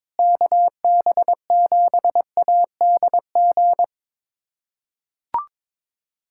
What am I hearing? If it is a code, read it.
Morse code, heard as a keyed tone, about 22 wpm, 700 Hz. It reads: KB7ADG
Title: USA Callsign DE Callsign - 6 Characters - 22wpm